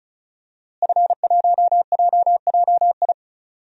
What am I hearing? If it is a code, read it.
Morse code, heard as a keyed tone, about 34 wpm, 700 Hz. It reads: F1JJI
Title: DXCC-100 Call Signs - 5 Characters - Repeated 3x - 35wpm